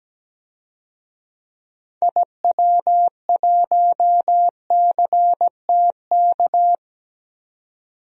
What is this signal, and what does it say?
Morse code, heard as a keyed tone, about 17 wpm, 700 Hz. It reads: IW1CTK
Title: DXCC-100 Call Signs - 6 Characters - 17wpm